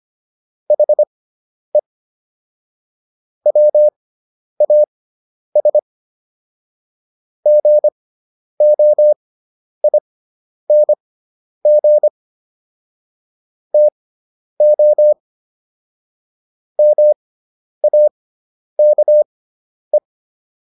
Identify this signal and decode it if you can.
Morse code, heard as a keyed tone, about 25 wpm, spaced as for 10 wpm, 600 Hz. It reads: HE WAS GOING TO MAKE